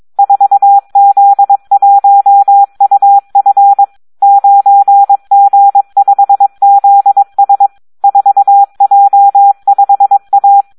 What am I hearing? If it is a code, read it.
Morse code, heard as a keyed tone, about 22 wpm, 800 Hz. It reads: 4Z1UF 9G5ZS 4J5A
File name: Audio practica CW / SD 1 22